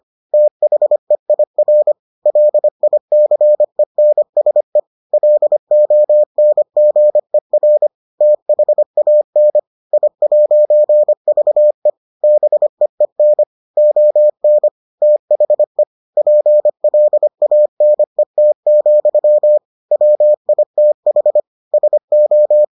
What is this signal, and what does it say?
Morse code, heard as a keyed tone, about 25 wpm, 600 Hz. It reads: THEIR LICENSE LONGER THAN I'VE BEEN ON THE PLANET, WITH SO